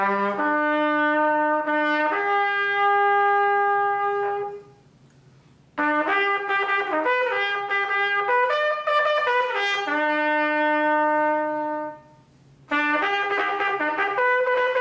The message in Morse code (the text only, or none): none